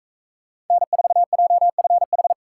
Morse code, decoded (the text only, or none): N4JFH